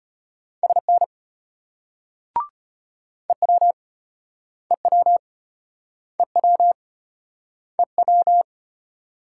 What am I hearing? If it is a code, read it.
Morse code, heard as a keyed tone, about 33 wpm, 700 Hz. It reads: SN EW EW EW EW